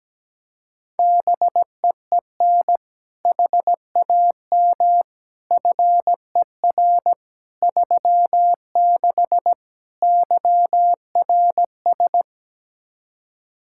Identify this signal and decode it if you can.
Morse code, heard as a keyed tone, about 17 wpm, 700 Hz. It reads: BEEN HAM FER 36 YRS
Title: Practice QSO - Exchange 3 - More Info - 17wpm